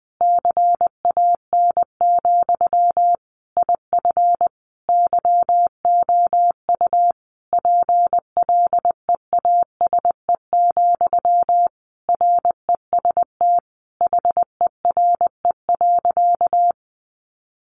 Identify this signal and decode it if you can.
Morse code, heard as a keyed tone, about 20 wpm, 700 Hz. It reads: CAN, IF YOU PLEASE, REST HERE.
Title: Sentences from Top 1500 Words 20wpm